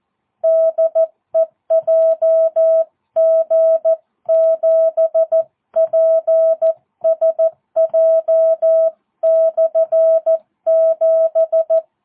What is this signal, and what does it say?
Morse code, heard as a keyed tone, about 14 wpm, 650 Hz. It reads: DE JG7PSJ/7
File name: jh8cbh / 2022cw